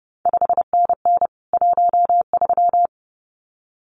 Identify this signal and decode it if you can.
Morse code, heard as a keyed tone, about 30 wpm, 700 Hz. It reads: 5NN 13
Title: CQ WW DX Contest Exchange - 30wpm